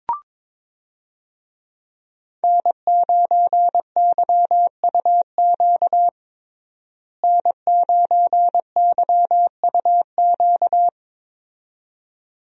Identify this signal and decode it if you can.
Morse code, heard as a keyed tone, about 22 wpm, 700 Hz. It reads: N9YUQ N9YUQ
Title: USA Call Signs - 5 Characters - Repeated 2x - 22wpm